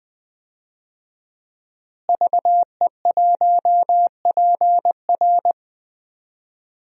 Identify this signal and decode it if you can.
Morse code, heard as a keyed tone, about 20 wpm, 700 Hz. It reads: VE1PR
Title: DXCC-100 Call Signs - 5 Characters - Repeated 3x - 20wpm